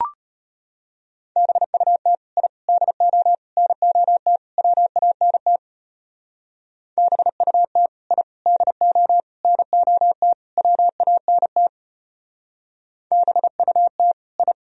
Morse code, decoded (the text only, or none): BUT I DO NOT WANT BUT I DO NOT WANT BUT I